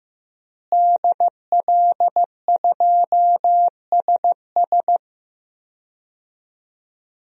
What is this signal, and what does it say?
Morse code, heard as a keyed tone, about 15 wpm, 700 Hz. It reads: DL2SS
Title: DXCC-100 Call Signs - 5 Characters - Repeated 3x - 15wpm